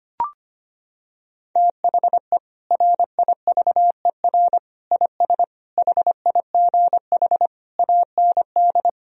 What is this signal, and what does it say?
Morse code, heard as a keyed tone, about 25 wpm, 700 Hz. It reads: THE RIVER IS HIGH AND
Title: Sentences from Top 1000 Words - Review Part 4 - 25wpm